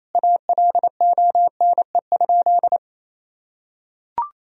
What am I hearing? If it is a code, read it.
Morse code, heard as a keyed tone, about 28 wpm, 700 Hz. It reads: ALONE?
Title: Sentences from Top 1000 Words - Review Part 2 - 28wpm